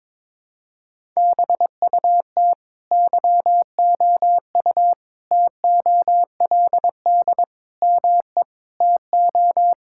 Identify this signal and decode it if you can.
Morse code, heard as a keyed tone, about 22 wpm, 700 Hz. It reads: BUT YOU TOLD ME TO
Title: Sentences from Top 1600 Words 22wpm